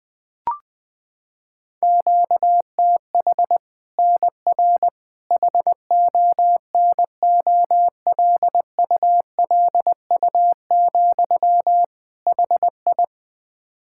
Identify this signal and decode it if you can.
Morse code, heard as a keyed tone, about 20 wpm, 700 Hz. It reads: QTH NR HONOLULU, HI
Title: Practice QSO - Exchange 1 - Beginning - 20wpm